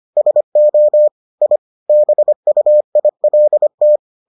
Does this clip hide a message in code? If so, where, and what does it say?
Yes: Morse code from 0.2 to 4.0 s, heard as a keyed tone, about 25 wpm, 600 Hz. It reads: SO I BUILT